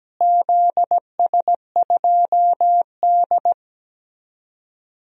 Morse code, heard as a keyed tone, about 17 wpm, 700 Hz. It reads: ZS2D
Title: DXCC-100 Call Signs - 4 Characters - Repeated 2x - 17wpm